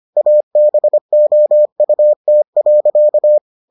Morse code, heard as a keyed tone, about 25 wpm, 600 Hz. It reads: ABOUT.